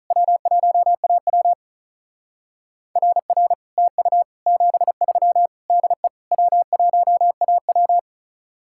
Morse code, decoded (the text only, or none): W1AW RR TU 73 DE W1AW